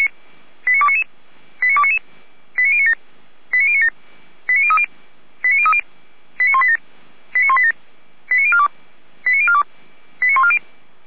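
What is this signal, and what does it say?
Five-tone selective call (ZVEI), heard as quick runs of five tones, tones of about 70 ms. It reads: 89299 89299 89097 89097 89030 89030 89187 89187 89042 89042 89140
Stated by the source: ZVEI five-tone call 89294 received by SDR.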